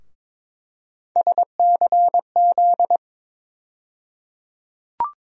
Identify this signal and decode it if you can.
Morse code, heard as a keyed tone, about 22 wpm, 700 Hz. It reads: SCZ
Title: USA Call Sign Postfixes - 3 Letters 22wpm